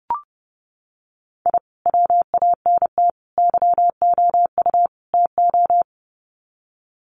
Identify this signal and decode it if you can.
Morse code, heard as a keyed tone, about 29 wpm, 700 Hz. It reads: I WANT YOU TO
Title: Sets of 4 Words 30wpm